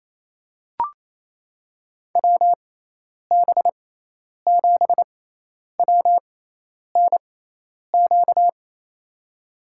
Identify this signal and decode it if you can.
Morse code, heard as a keyed tone, about 28 wpm, spaced as for 10 wpm, 700 Hz. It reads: WB7WNQ
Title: USA Call Signs - 6 Characters - 6x Character Spacing - 28wpm